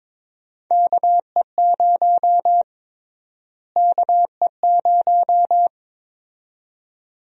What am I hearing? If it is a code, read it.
Morse code, heard as a keyed tone, about 22 wpm, 700 Hz. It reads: KE0 KE0
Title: USA Call Sign Prefixes - Repeated 2x - 22wpm